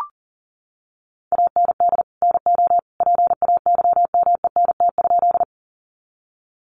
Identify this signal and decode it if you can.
Morse code, heard as a keyed tone, about 40 wpm, 700 Hz. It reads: AND NO PAYMENT?
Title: Sentences from Top 1200 Words 40wpm